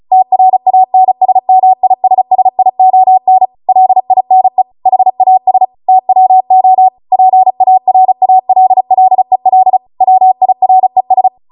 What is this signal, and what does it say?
Morse code, heard as a keyed tone, about 36 wpm, 750 Hz. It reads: TRANSMISSION LINE HAS TWO PARALLEL WIRES